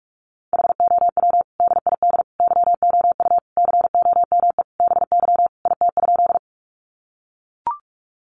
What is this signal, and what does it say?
Morse code, heard as a keyed tone, about 44 wpm, 700 Hz. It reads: HOW DID YOU COME BY IT?
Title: Sentences from Top 100 Words 45wpm